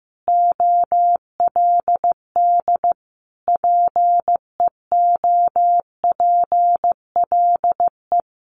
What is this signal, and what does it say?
Morse code, heard as a keyed tone, about 15 wpm, 700 Hz. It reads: OLD PEOPLE
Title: Sentences from Top 1300 Words 15wpm